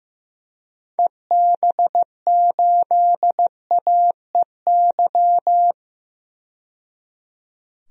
Morse code, heard as a keyed tone, about 16 wpm, 700 Hz. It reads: EB8AEY
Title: DXCC-100 Call Signs - 6 Characters - 15wpm